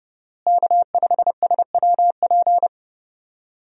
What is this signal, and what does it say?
Morse code, heard as a keyed tone, about 30 wpm, 700 Hz. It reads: K5SWP
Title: USA Call Signs - 5 Characters - Repeated 2x - 30wpm